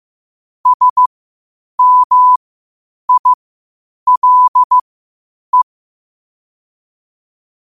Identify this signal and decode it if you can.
Morse code, heard as a keyed tone, about 15 wpm, spaced as for 8 wpm, 1000 Hz. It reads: SMILE
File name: qotdfiles / 8wpm